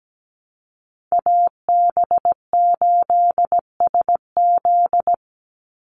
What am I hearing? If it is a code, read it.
Morse code, heard as a keyed tone, about 17 wpm, 700 Hz. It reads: AB8SZ
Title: USA Call Signs 17wpm